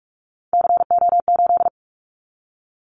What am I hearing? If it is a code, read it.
Morse code, heard as a keyed tone, about 44 wpm, 700 Hz. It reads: CO8
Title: Callsign Prefixes for Top 100 DXCC - 45wpm